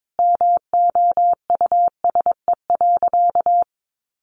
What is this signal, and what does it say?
Morse code, heard as a keyed tone, about 22 wpm, 700 Hz. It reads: MOUSE.
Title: A Sound Of Thunder 22wpm